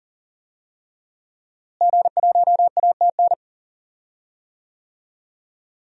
Morse code, heard as a keyed tone, about 40 wpm, 700 Hz. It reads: G1ATN
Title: DXCC-100 Call Signs - 5 Characters - Repeated 3x - 40wpm